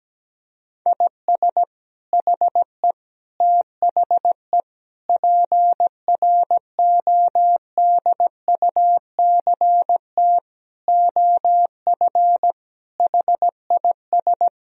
Morse code, heard as a keyed tone, about 17 wpm, 700 Hz. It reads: IS HE THE PRODUCT OF HIS
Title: Sentences from Top 1000 Words - Review Part 2 - 17wpm